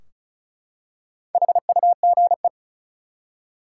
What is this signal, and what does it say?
Morse code, heard as a keyed tone, about 34 wpm, 700 Hz. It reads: HUGE
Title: Four Letter Words - 35wpm